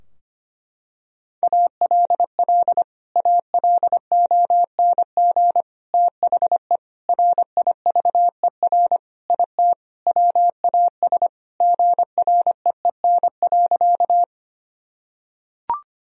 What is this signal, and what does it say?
Morse code, heard as a keyed tone, about 25 wpm, 700 Hz. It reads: ALL ALONG THE RIVER IT WAS GREEN.
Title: Sentences from Top 1000 Words - Review Part 2 - 25wpm